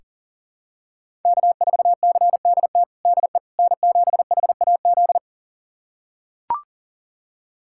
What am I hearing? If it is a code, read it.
Morse code, heard as a keyed tone, about 37 wpm, 700 Hz. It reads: K4CDT DE N7HAZ
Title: USA Callsign DE Callsign - 5 Characters - 40wpm